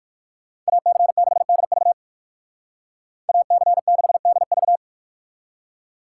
Morse code, heard as a keyed tone, about 60 wpm, 700 Hz. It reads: AC6DV AC6DV